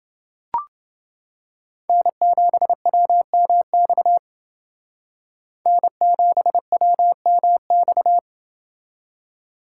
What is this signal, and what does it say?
Morse code, heard as a keyed tone, about 30 wpm, 700 Hz. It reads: N7WMX N7WMX